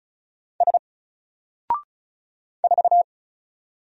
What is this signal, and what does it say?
Morse code, heard as a keyed tone, about 34 wpm, 700 Hz. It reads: S 4
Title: Single Letter-Number 35wpm